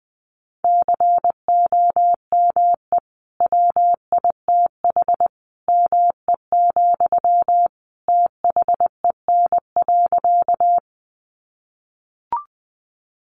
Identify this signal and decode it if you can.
Morse code, heard as a keyed tone, about 20 wpm, 700 Hz. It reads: COME WITH ME, THEN.
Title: Sentences from Top 200 Words 20wpm